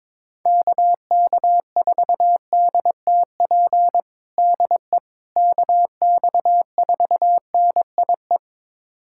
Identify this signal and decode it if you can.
Morse code, heard as a keyed tone, about 22 wpm, 700 Hz. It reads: KK4DTP DE KX4NIE
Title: USA Callsign DE Callsign - 6 Characters - 22wpm